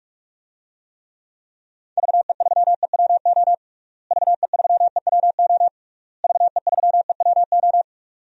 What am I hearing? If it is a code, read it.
Morse code, heard as a keyed tone, about 46 wpm, 700 Hz. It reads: VE3EWO VE3EWO VE3EWO
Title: DXCC-100 Call Signs - 6 Characters - Repeated 3x - 45wpm